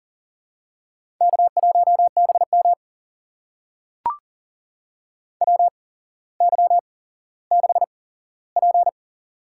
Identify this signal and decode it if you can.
Morse code, heard as a keyed tone, about 40 wpm, spaced as for 17 wpm, 700 Hz. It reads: K1BM WY6P
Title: USA Call Signs - 8x Character Spacing - 40wpm